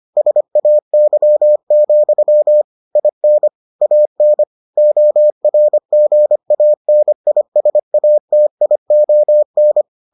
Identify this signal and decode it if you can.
Morse code, heard as a keyed tone, about 25 wpm, 600 Hz. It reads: SAY, IN AN ORGANISATION